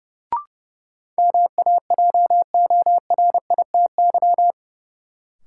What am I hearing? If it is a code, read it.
Morse code, heard as a keyed tone, about 30 wpm, 700 Hz. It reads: MAJORITY
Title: Sets of 3 Words - New Word Drill - 30wpm